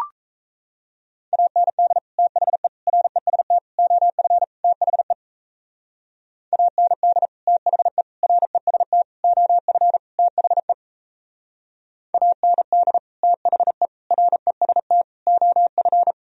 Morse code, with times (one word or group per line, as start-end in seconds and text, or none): 1.3-2.0 AND
2.2-2.7 THE
2.9-3.6 REST
3.8-4.4 OF
4.6-5.1 THE
6.5-7.3 AND
7.5-8.0 THE
8.2-9.0 REST
9.2-10.0 OF
10.2-10.7 THE
12.1-13.0 AND
13.2-13.9 THE
14.1-15.0 REST
15.3-16.1 OF